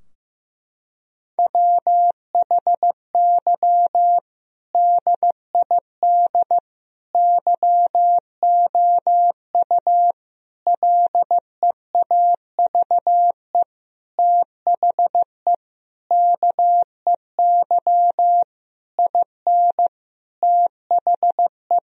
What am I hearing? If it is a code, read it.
Morse code, heard as a keyed tone, about 15 wpm, 700 Hz. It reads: WHY DID YOU LEAVE THE KEY IN THE
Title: Sentences from Top 1000 Words - Review Part 4 - 15wpm